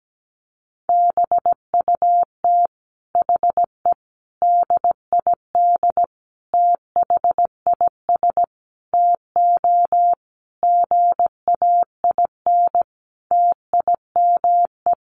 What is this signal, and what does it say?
Morse code, heard as a keyed tone, about 17 wpm, 700 Hz. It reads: BUT HE DID THIS TO GAIN TIME
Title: Sentences from Top 1400 Words 17wpm